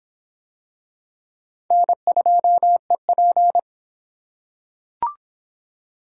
Morse code, heard as a keyed tone, about 24 wpm, 700 Hz. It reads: N2EP